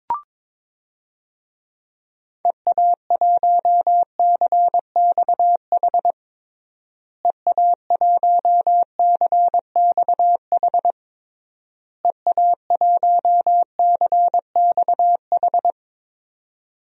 Morse code, heard as a keyed tone, about 22 wpm, 700 Hz. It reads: EA1CXH EA1CXH EA1CXH
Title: DXCC-100 Call Signs - 6 Characters - Repeated 3x - 22wpm